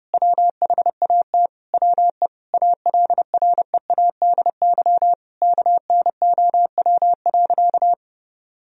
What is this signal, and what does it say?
Morse code, heard as a keyed tone, about 31 wpm, 700 Hz. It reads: WHAT WE ALREADY KNOW.
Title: Sentences from Top 1000 Words - Review Part 1 - 30wpm